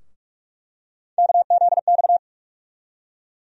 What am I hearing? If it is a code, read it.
Morse code, heard as a keyed tone, about 46 wpm, 700 Hz. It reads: KZX
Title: USA Call Sign Postfixes - 3 Letters 45wpm